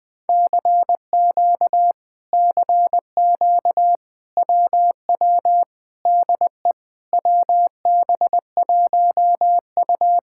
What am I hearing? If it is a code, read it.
Morse code, heard as a keyed tone, about 20 wpm, 700 Hz. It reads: CQ CQ WW DE WB1U
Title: Speed-Racing Common QSO Phrases Encore - 17wpm